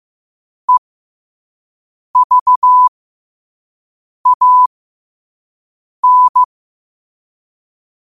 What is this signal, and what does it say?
Morse code, heard as a keyed tone, about 15 wpm, spaced as for 5 wpm, 1000 Hz. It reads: EVAN
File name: qotdfiles / 5wpm